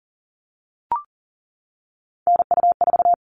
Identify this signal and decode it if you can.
Morse code, heard as a keyed tone, about 40 wpm, 700 Hz. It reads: NU4
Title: All Cut Numbers - 3-Digit 40wpm